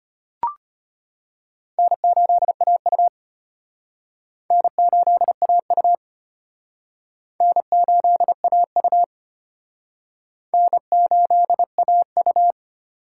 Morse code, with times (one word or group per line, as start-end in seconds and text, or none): 1.8-3.1 N8AU
4.5-5.9 N8AU
7.4-9.0 N8AU
10.5-12.5 N8AU